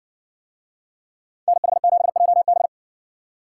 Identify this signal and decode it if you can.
Morse code, heard as a keyed tone, about 56 wpm, 700 Hz. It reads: NH7JB